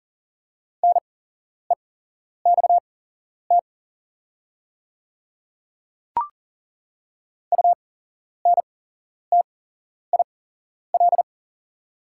Morse code, heard as a keyed tone, about 40 wpm, spaced as for 11 wpm, 690 Hz. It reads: NEXT UNTIL
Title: 8x Character Spacing - Top 300 Words - 40wpm